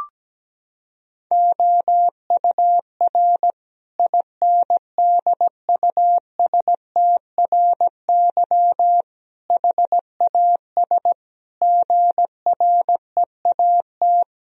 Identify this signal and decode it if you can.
Morse code, heard as a keyed tone, about 17 wpm, 700 Hz. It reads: OUR INDUSTRY HAS GREAT
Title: Sentences from Top 1000 Words - Review Part 4 - 17wpm